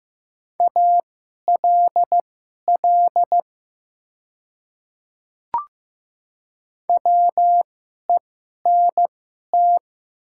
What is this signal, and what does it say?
Morse code, heard as a keyed tone, about 15 wpm, spaced as for 11 wpm, 700 Hz. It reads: ALL WENT
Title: 2x Character Spacing - Top 300 Words - 15wpm